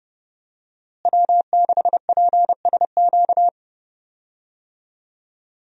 Morse code, heard as a keyed tone, about 30 wpm, 700 Hz. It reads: W6PSQ
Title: USA Call Signs - 5 Characters - Repeated 3x - 30wpm